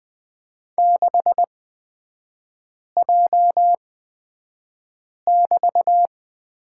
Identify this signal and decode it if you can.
Morse code, heard as a keyed tone, about 20 wpm, spaced as for 5 wpm, 700 Hz. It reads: 6J<BT>
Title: 255 Lesson 41e - Period - Three Characters 20wpm